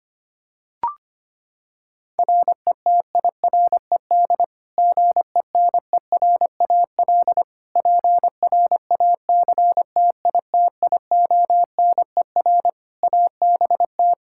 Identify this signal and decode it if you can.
Morse code, heard as a keyed tone, about 25 wpm, 700 Hz. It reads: RETIRED GENERAL PRACTITIONER ABT
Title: Practice QSO - Exchange 3 - More Info - 25wpm